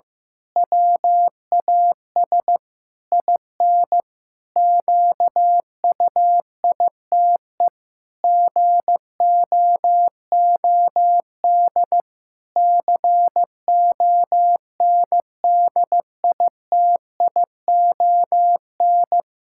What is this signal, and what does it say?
Morse code, heard as a keyed tone, about 15 wpm, 700 Hz. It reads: WAS IN QUITE GOOD CONDITION
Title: Pangram Challenge v2 - 15wpm